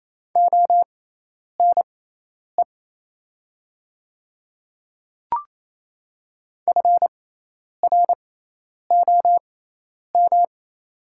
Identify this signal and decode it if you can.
Morse code, heard as a keyed tone, about 28 wpm, spaced as for 10 wpm, 700 Hz. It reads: ONE FROM
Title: 6x Character Spacing - Top 100 Words - 28wpm